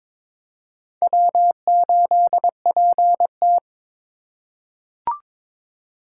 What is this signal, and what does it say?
Morse code, heard as a keyed tone, about 22 wpm, 700 Hz. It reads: W8PT
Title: USA Call Signs - 4 Characters - 3x Character Spacing - 22wpm